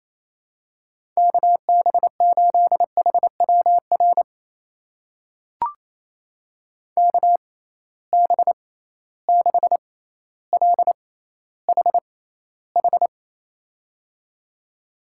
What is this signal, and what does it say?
Morse code, heard as a keyed tone, about 28 wpm, spaced as for 14 wpm, 700 Hz. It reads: KB8HWR KB6LHH
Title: USA Call Signs - 6 Characters - 6x Character Spacing - 28wpm